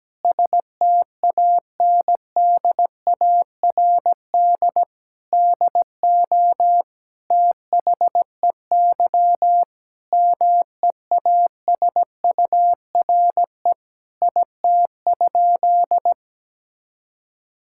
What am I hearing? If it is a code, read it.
Morse code, heard as a keyed tone, about 17 wpm, 700 Hz. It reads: STANDARD DO THEY MEASURE IT?
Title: Sentences from Top 1000 Words - Review Part 6 - 17wpm